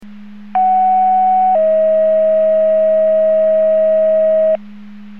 Two-tone page, heard as a one-second tone, then a three-second tone, about 750 then 640 Hz. It is heard from 0.6 to 4.6 s.
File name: Portage 154.385 Paging